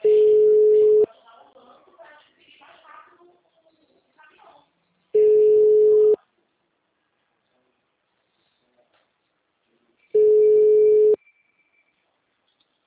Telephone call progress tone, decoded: ringback tone

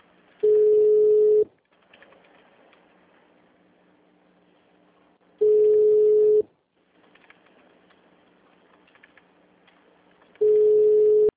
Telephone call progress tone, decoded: ringback tone

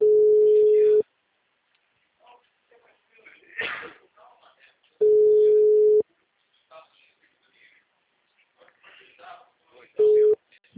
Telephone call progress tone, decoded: ringback tone

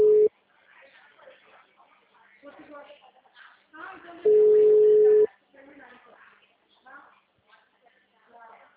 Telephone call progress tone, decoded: ringback tone

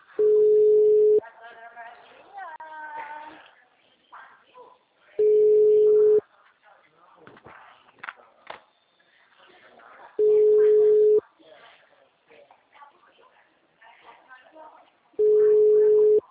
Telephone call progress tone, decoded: ringback tone

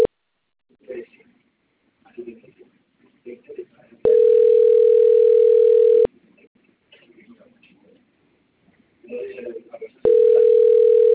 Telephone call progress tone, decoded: ringback tone

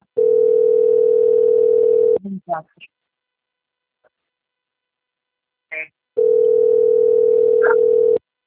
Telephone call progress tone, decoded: ringback tone

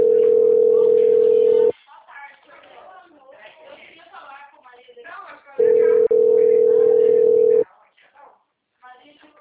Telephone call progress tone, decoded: ringback tone

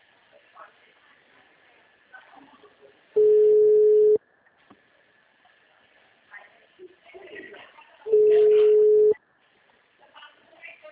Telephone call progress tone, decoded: ringback tone

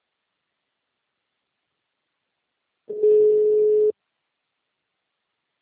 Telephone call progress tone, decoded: ringback tone

